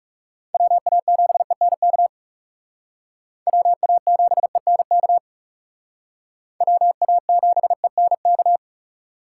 Morse code, heard as a keyed tone, about 35 wpm, 700 Hz. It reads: WA7ENK WA7ENK WA7ENK
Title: Speed Racing - USA 2x3 Call Signs 30wpm